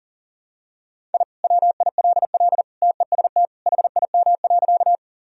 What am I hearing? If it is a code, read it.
Morse code, heard as a keyed tone, about 40 wpm, 700 Hz. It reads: I WILL TEST HIM.
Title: Sentences from Top 1000 Words - Review Part 3 - 40wpm